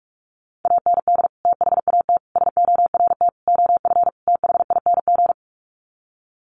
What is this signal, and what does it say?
Morse code, heard as a keyed tone, about 44 wpm, 690 Hz. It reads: AND THAT SORT OF THING
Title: Sets of 5 Words Encore2 45wpm